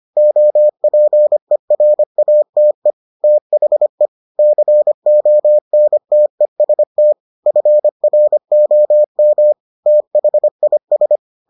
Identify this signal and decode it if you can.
Morse code, heard as a keyed tone, about 25 wpm, 600 Hz. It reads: OPERATE THE CONTEST FROM THIS